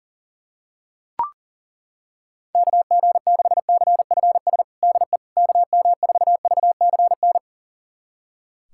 Morse code, heard as a keyed tone, about 40 wpm, 700 Hz. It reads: KG6CFS DE KM4VCN
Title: USA Callsign DE Callsign - 6 Characters - 40wpm